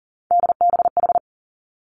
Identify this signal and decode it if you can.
Morse code, heard as a keyed tone, about 40 wpm, 700 Hz. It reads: DBH
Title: Three Challenging Letter-Number Combinations 40wpm